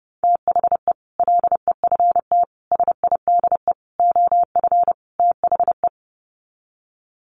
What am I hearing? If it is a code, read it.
Morse code, heard as a keyed tone, about 30 wpm, 700 Hz. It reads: THE LEFT SIDE OF THE
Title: Sets of 5 Words Encore 30wpm